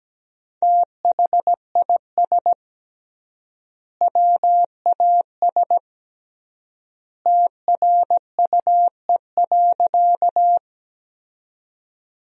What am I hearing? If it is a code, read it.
Morse code, heard as a keyed tone, about 17 wpm, 700 Hz. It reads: THIS WAS TRUE.
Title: 3x Word-Spacing - Sentences from Top 400 Words - 17wpm